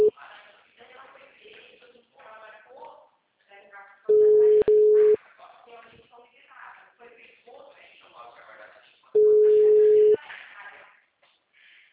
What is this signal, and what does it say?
Ringback tone (Europe, 425 Hz), with speech.
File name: out- -2002